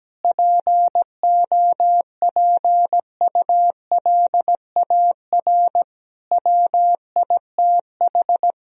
Morse code, heard as a keyed tone, about 17 wpm, 700 Hz. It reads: POPULAR WITH
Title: Sentences from Top 1000 Words - Review Part 3 - 17wpm